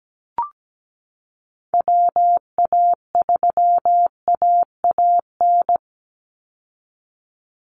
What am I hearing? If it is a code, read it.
Morse code, heard as a keyed tone, about 17 wpm, 700 Hz. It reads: WA3AAN
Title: Call Signs Repeated 2x - 17wpm